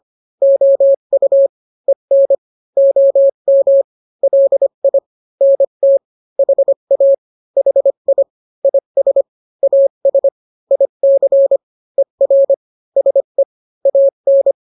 Morse code, heard as a keyed tone, about 25 wpm, 550 Hz. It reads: OU EN OM LI NT HA HI IS AS IC ER SE AN